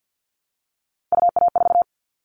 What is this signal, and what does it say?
Morse code, heard as a keyed tone, about 50 wpm, 700 Hz. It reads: UA4
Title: Callsign Prefixes for Top 50 DXCC - 50wpm